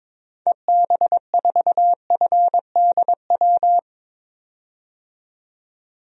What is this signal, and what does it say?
Morse code, heard as a keyed tone, about 22 wpm, 700 Hz. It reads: EB4FDW
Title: DXCC-100 Call Signs - 6 Characters - 22wpm